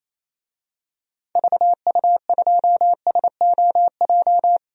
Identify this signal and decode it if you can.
Morse code, heard as a keyed tone, about 28 wpm, 700 Hz. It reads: VU2SOJ
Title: DXCC-100 Call Signs - 6 Characters - Repeated 2x - 28wpm